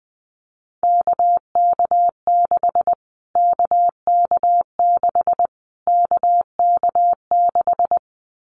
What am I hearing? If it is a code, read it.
Morse code, heard as a keyed tone, about 20 wpm, 700 Hz. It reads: KK6 KK6 KK6
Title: Callsign Prefixes for Top 25 DXCC - Repeated 3x - 20wpm